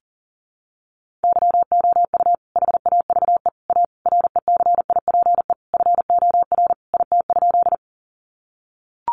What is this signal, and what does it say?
Morse code, heard as a keyed tone, about 40 wpm, 700 Hz. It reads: YOU HAVE A RECIPE FOR IT?
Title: Sentences from Top 1200 Words 40wpm